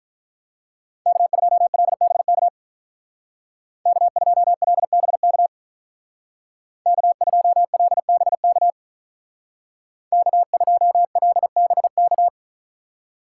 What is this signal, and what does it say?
Morse code, heard as a keyed tone, about 44 wpm, 700 Hz. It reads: K2LBK K2LBK K2LBK K2LBK